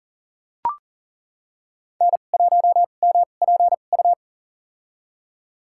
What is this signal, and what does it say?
Morse code, heard as a keyed tone, about 37 wpm, spaced as for 28 wpm, 700 Hz. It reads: N1MPU
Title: USA Call Signs - 5 Characters - 2x Character Spacing - 40wpm